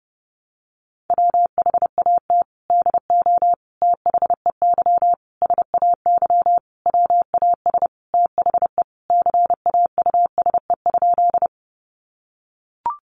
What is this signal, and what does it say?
Morse code, heard as a keyed tone, about 30 wpm, 700 Hz. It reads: WHAT DO THEY SAY WAS THE CAUSE?
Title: Sentences from Top 200 Words 30wpm